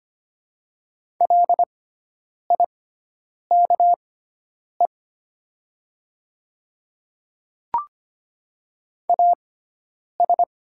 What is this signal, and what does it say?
Morse code, heard as a keyed tone, about 25 wpm, spaced as for 9 wpm, 700 Hz. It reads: LIKE AS
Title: 6x Character Spacing - Top 200 Words - 25wpm